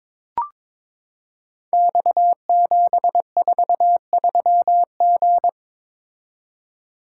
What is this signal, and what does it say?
Morse code, heard as a keyed tone, about 22 wpm, 700 Hz. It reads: X743G
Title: Code Groups - 5 Letter:Numbers - 22wpm